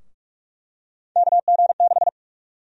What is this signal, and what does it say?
Morse code, heard as a keyed tone, about 43 wpm, 700 Hz. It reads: KG6
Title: USA Call Sign Prefixes - 45wpm